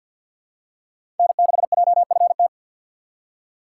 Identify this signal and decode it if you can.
Morse code, heard as a keyed tone, about 51 wpm, 700 Hz. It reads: N6JFT
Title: USA Call Signs - 5 Characters - 2x Character Spacing - 50wpm